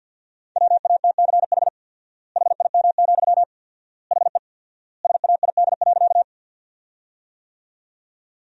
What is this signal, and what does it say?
Morse code, heard as a keyed tone, about 52 wpm, 700 Hz. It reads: WATCH HIM, HE SAID.
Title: 4x Word-Spacing - Sentences from Top 300 Words - 50wpm